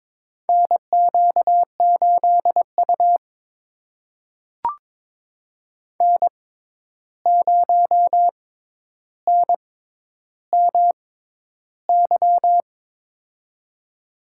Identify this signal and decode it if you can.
Morse code, heard as a keyed tone, about 22 wpm, spaced as for 11 wpm, 700 Hz. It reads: NQ8U N0NMY
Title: USA Call Signs - 6x Character Spacing - 22wpm